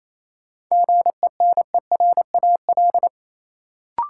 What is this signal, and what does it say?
Morse code, heard as a keyed tone, about 28 wpm, 700 Hz. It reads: GENERAL
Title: Sentences from Top 600 Words - New Word Drill - 28wpm